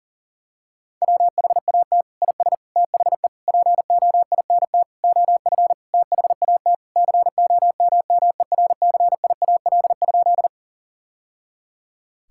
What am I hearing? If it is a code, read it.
Morse code, heard as a keyed tone, about 40 wpm, 700 Hz. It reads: WHAT IS THE POINT OF THAT COMMERCIAL?
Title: Sentences from Top 1000 Words - Review Part 5 - 40wpm